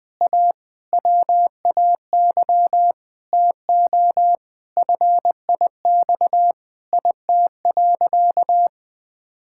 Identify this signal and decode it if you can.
Morse code, heard as a keyed tone, about 20 wpm, 700 Hz. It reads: A WAY TO FIX IT.
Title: Sentences with 6-Word Phrases - 20wpm